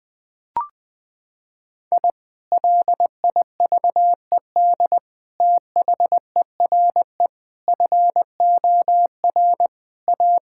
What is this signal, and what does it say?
Morse code, heard as a keyed tone, about 21 wpm, 700 Hz. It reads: I LIVED THERE FOR A
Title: Sentences with 3-Word Phrases - 20wpm